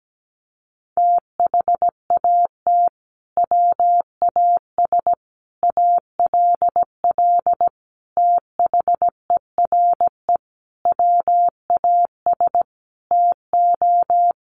Morse code, heard as a keyed tone, about 17 wpm, 700 Hz. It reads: THAT WAS ALL THERE WAS TO
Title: Sentences from Top 1100 Words 17wpm